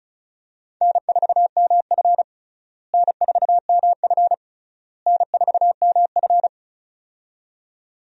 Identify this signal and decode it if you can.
Morse code, heard as a keyed tone, about 34 wpm, 700 Hz. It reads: N4MF N4MF N4MF
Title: USA Call Signs - 4 Characters - Repeated 3x - 35wpm